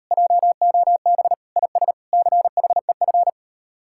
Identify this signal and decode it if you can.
Morse code, heard as a keyed tone, about 36 wpm, 700 Hz. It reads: JOB IS CHEF